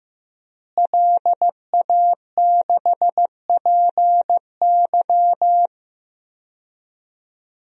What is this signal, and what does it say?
Morse code, heard as a keyed tone, about 15 wpm, 700 Hz. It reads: LA6PY
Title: DXCC-100 Call Signs - 5 Characters - Repeated 2x - 15wpm